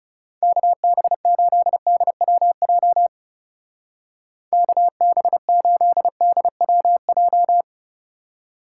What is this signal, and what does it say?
Morse code, heard as a keyed tone, about 36 wpm, 700 Hz. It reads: KB8DWJ KB8DWJ
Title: Speed Racing - USA 2x3 Call Signs 30wpm